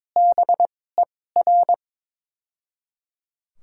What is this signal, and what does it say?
Morse code, heard as a keyed tone, about 22 wpm, spaced as for 16 wpm, 700 Hz. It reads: BER
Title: 2x Character Spacing - Top 100 Words - 22wpm